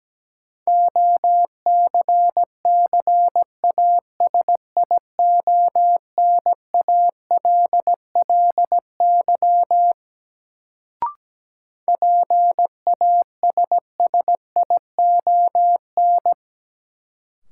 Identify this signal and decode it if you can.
Morse code, heard as a keyed tone, about 17 wpm, 700 Hz. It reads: OCCASIONALLY PASSION
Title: Sentences from Top 1300 Words - New Word Drill Alt - 17wpm